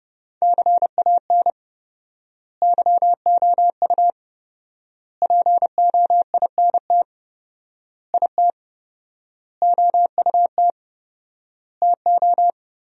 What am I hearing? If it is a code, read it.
Morse code, heard as a keyed tone, about 30 wpm, 700 Hz. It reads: CAN YOU POINT IT OUT TO
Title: 4x Word-Spacing - Sentences from Top 200 Words - 30wpm